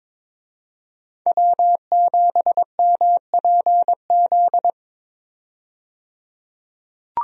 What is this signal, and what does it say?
Morse code, heard as a keyed tone, about 21 wpm, 700 Hz. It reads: W7MPZ
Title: USA Call Signs - 5 Characters - 22wpm